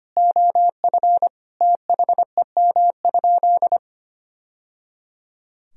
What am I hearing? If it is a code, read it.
Morse code, heard as a keyed tone, about 25 wpm, 700 Hz. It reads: OF THEM?
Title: Sentences from Top 1000 Words - Review Part 2 - 25wpm